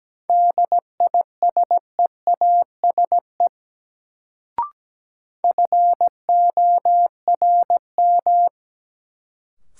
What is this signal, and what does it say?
Morse code, heard as a keyed tone, about 17 wpm, 700 Hz. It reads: DISEASE FORM
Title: Sentences from Top 1100 Words - New Word Drill Alt - 17wpm